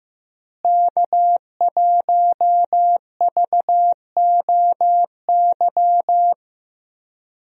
Morse code, heard as a keyed tone, about 15 wpm, 700 Hz. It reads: K1VOY
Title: USA Call Signs - 5 Characters - Repeated 2x - 15wpm